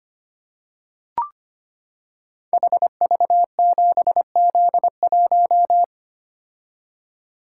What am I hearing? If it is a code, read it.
Morse code, heard as a keyed tone, about 25 wpm, 700 Hz. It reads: HV7Z1
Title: Code Groups - 5 Letter:Numbers - 25wpm